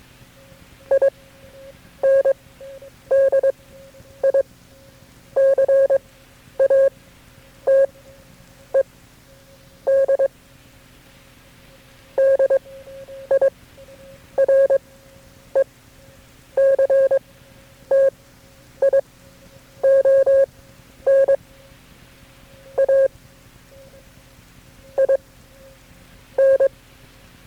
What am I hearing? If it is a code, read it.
Morse code, heard as a keyed tone, about 22 wpm, spaced as for 9 wpm, 540 Hz. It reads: INDICATED DIRECTION A I N
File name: KB6MT-tape3-side1